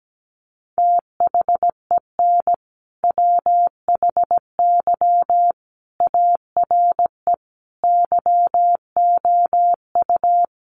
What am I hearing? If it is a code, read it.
Morse code, heard as a keyed tone, about 17 wpm, 700 Hz. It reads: THEN WHY ARE YOU
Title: Sentences from Top 200 Words 17wpm